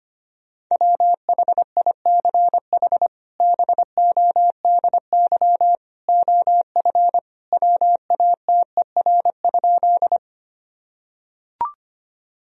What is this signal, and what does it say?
Morse code, heard as a keyed tone, about 25 wpm, 700 Hz. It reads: WHICH BODY OF WATER?
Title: Sentences from Top 1000 Words - Review Part 2 - 25wpm